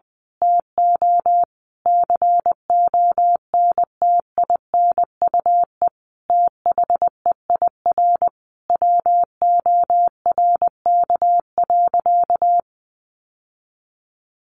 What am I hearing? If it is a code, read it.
Morse code, heard as a keyed tone, about 19 wpm, 700 Hz. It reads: TO CONTINUE THEIR WORK.
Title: Sentences from Top 1100 Words 20wpm